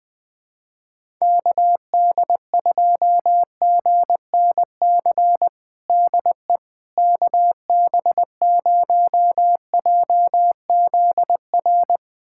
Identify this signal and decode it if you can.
Morse code, heard as a keyed tone, about 20 wpm, 700 Hz. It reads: KD2GNC DE KB0JZR
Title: USA Callsign DE Callsign - 6 Characters - 20wpm